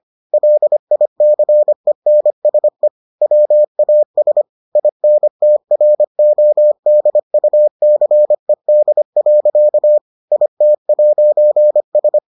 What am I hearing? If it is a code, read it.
Morse code, heard as a keyed tone, about 25 wpm, 600 Hz. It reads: LICENSE WAS INTRODUCED. IT'S